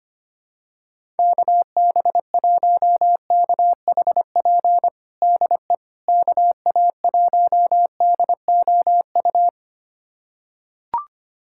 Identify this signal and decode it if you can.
Morse code, heard as a keyed tone, about 25 wpm, 700 Hz. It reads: KB1KHP DE KA1DOU
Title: USA Callsign DE Callsign 25wpm